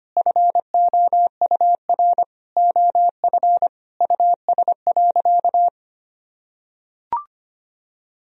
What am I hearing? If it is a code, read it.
Morse code, heard as a keyed tone, about 25 wpm, 700 Hz. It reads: FOUR OF US.
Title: Sentences from Top 1000 Words - Review Part 1 - 25wpm